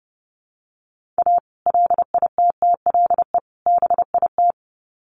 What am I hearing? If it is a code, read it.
Morse code, heard as a keyed tone, about 30 wpm, 700 Hz. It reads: A LITTLE BIT
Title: Sets of 3 Words 30wpm